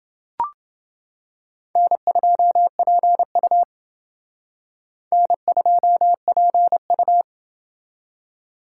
Morse code, heard as a keyed tone, about 28 wpm, 700 Hz. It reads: N2PU N2PU